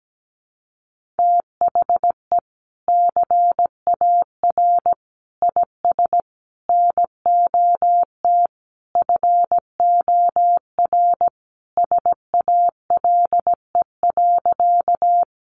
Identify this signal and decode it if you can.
Morse code, heard as a keyed tone, about 17 wpm, 700 Hz. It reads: THE CAR IS NOT FOR SALE.
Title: Sentences from Top 1400 Words 17wpm